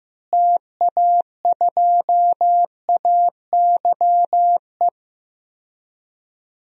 Morse code, heard as a keyed tone, about 15 wpm, 700 Hz. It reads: TA2AYE
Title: DXCC-100 Call Signs - 6 Characters - Repeated 2x - 15wpm